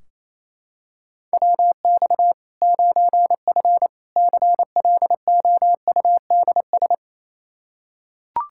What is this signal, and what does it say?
Morse code, heard as a keyed tone, about 28 wpm, 700 Hz. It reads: WX 9F CLOUDS
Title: K1USN SST QSO Practice 28wpm